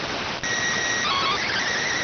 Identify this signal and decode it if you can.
SSTV (slow-scan television), PD 290.